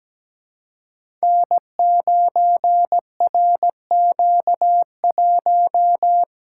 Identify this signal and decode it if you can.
Morse code, heard as a keyed tone, about 17 wpm, 700 Hz. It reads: N9RQ1
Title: Code Groups - 5 Letter:Numbers - 17wpm